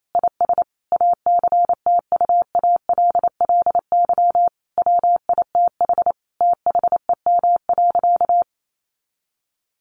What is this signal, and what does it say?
Morse code, heard as a keyed tone, about 28 wpm, 700 Hz. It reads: IS ACTUALLY WITH THEM.
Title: Sentences from Top 700 Words 28wpm